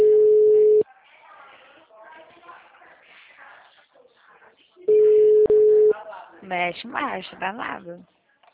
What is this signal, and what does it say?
Ringback tone (Europe, 425 Hz), with speech.
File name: out- -2013- - - .40263